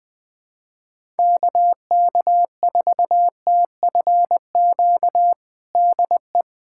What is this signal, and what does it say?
Morse code, heard as a keyed tone, about 21 wpm, 700 Hz. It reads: KK4TFQ DE
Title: USA Callsign DE Callsign - 6 Characters - 20wpm